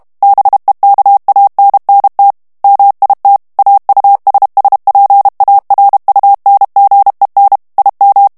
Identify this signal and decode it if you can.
Morse code, heard as a keyed tone, about 32 wpm, 800 Hz. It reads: BEKANNT MIT AUSSPARUNGEN IM
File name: ALLEMAND texte clair 1 vitesse 30 mots minute